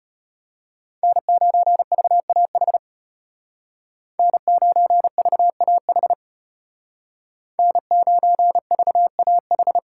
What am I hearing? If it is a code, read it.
Morse code, heard as a keyed tone, about 32 wpm, 700 Hz. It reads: N9VAH N9VAH N9VAH